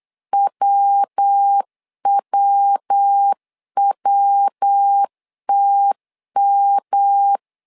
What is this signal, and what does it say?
Morse code, heard as a keyed tone, about 8 wpm, 800 Hz. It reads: WWWTM
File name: LEZIONI / 3 MWRO / 3 MW 8wpm 2501